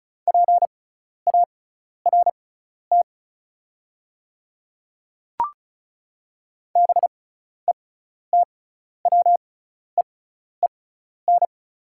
Morse code, heard as a keyed tone, about 36 wpm, spaced as for 12 wpm, 690 Hz. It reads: PART BETWEEN
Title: 6x Character Spacing - Top 300 Words - 35wpm